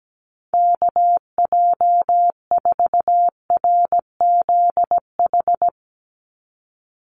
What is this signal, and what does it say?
Morse code, heard as a keyed tone, about 17 wpm, 700 Hz. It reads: KJ4RZH
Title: USA Call Signs 17wpm